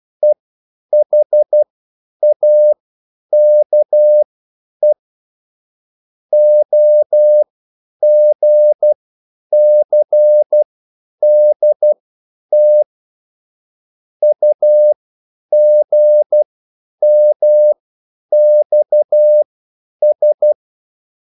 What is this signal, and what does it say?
Morse code, heard as a keyed tone, about 12 wpm, spaced as for 9 wpm, 600 Hz. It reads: EHAKE OGCDT UGMXS